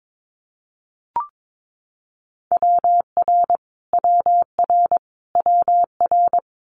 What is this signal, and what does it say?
Morse code, heard as a keyed tone, about 22 wpm, 700 Hz. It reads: WR WR WR
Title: Multi-letter Phonemes Repeated 3x 22wpm